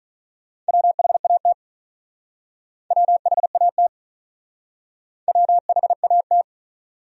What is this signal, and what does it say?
Morse code, heard as a keyed tone, about 40 wpm, 700 Hz. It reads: WHAT WHAT WHAT
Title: Speed-Racing Top 100 Words 35wpm